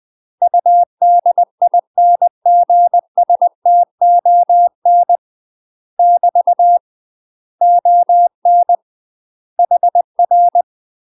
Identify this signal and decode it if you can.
Morse code, heard as a keyed tone, about 20 wpm, 700 Hz. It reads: UDINGSTON <BT> ON HR
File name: UK MASTER QSOs TRAINING/20WPM